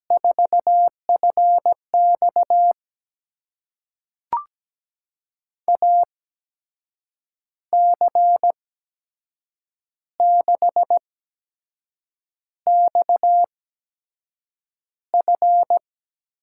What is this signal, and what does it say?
Morse code, heard as a keyed tone, about 17 wpm, spaced as for 6 wpm, 700 Hz. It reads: 4FX AC6XF
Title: USA Call Signs - 5 Characters - 8x Character Spacing - 17wpm